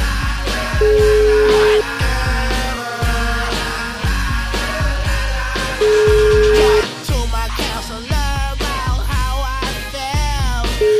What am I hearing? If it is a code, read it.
Ringback tone (Europe, 425 Hz), with noise.